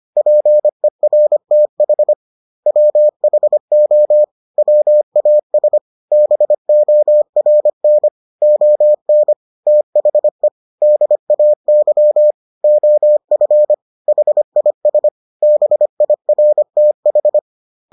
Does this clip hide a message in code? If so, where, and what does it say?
Yes: Morse code from 0.2 to 17.4 s, heard as a keyed tone, about 25 wpm, 600 Hz. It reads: PERTH WHO WAS BORN ON THE DAY OF HIS BIRTH